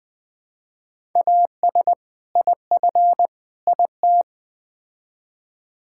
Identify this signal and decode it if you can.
Morse code, heard as a keyed tone, about 20 wpm, 700 Hz. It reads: AS IF IT
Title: Build A Phrase - 4-Word Phrases - 20wpm